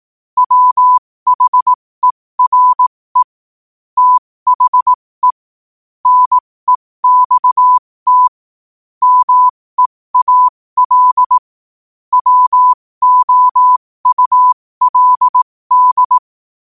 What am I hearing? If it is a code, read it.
Morse code, heard as a keyed tone, about 18 wpm, spaced as for 15 wpm, 1000 Hz. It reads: WHERE THE NEXT MEAL WOULD
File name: qotdfiles / 15wpm